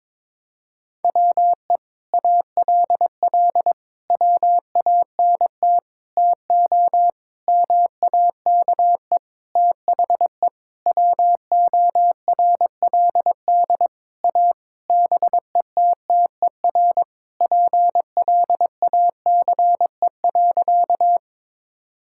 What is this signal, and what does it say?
Morse code, heard as a keyed tone, about 22 wpm, 700 Hz. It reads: WE ALL WANT TO MAKE THE WORLD A BETTER PLACE.
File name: Sentences with 7-Word Phrases - 22wpm